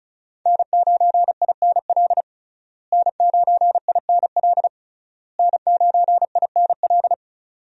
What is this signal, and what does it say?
Morse code, heard as a keyed tone, about 34 wpm, 700 Hz. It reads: N9INL N9INL N9INL
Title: USA Call Signs - 5 Characters - Repeated 3x - 35wpm